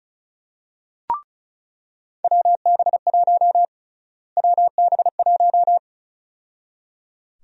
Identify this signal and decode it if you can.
Morse code, heard as a keyed tone, about 34 wpm, 700 Hz. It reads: WB1 WB1
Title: USA Call Sign Prefixes - Repeated 2x - 35wpm